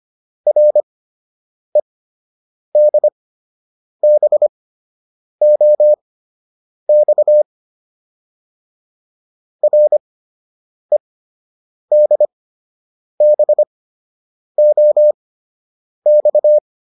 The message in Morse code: REDBOX REDBOX

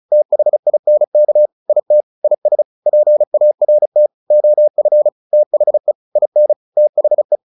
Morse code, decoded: THINK IT IS PART OF THE IN THE

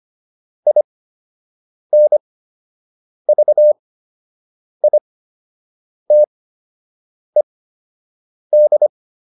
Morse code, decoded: INVITED

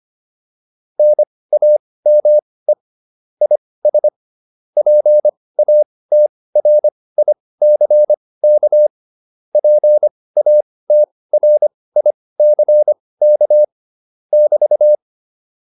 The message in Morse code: NAME IS PATRICK PATRICK <BT>

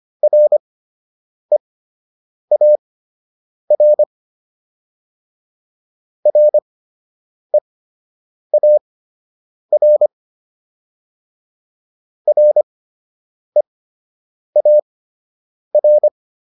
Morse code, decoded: REAR REAR REAR

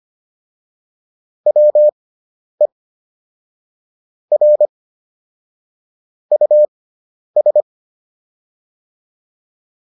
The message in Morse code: WE R US

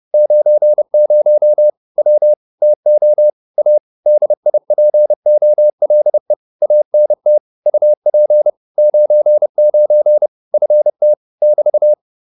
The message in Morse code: 90 W TO A DIPOLE ANT UP 99 FT <BT>

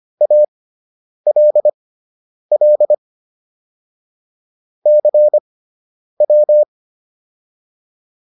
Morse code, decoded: ALL CW

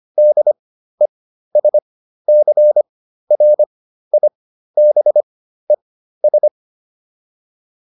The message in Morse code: DESCRIBES

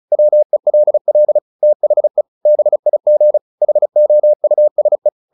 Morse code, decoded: WELL THE BIG HOUSE

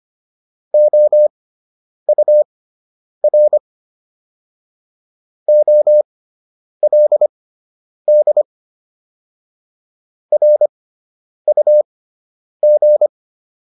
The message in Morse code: OUR OLD RUG

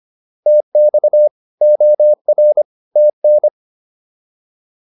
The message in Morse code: TX OR TN

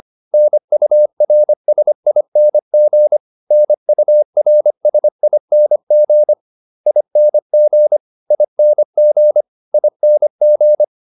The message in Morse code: NURSING NURSING ING ING ING

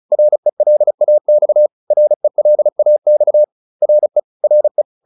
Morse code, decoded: RELAX RELAX RE RE

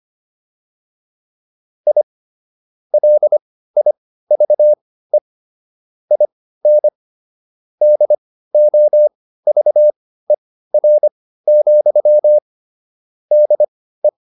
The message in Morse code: I LIVE IN DOVER, DE

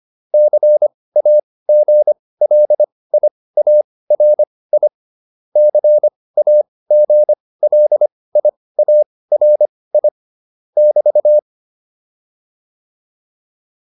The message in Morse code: CAGLIARI CAGLIARI <BT>

